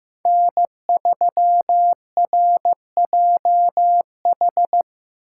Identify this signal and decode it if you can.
Morse code, heard as a keyed tone, about 15 wpm, 700 Hz. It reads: N3RJH
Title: USA Call Signs - 5 Characters - 2x Character Spacing - 15wpm